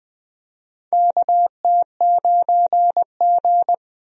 Morse code, heard as a keyed tone, about 21 wpm, 700 Hz. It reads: KT9G
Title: USA Call Signs - 4 Characters - Repeated 2x - 20wpm